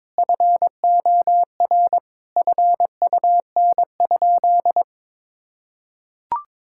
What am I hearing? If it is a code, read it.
Morse code, heard as a keyed tone, about 22 wpm, 700 Hz. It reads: FOR FUN?
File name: practice / 22wpm / Sentences with 4-Word Phrases - 22wpm